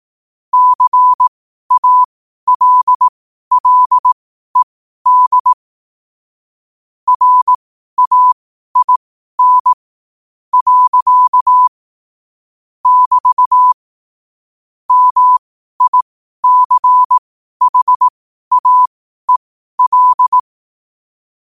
Morse code, heard as a keyed tone, about 18 wpm, spaced as for 12 wpm, 1000 Hz. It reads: CALLED RAIN. <BT> MICHAEL
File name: qotdfiles / 12wpm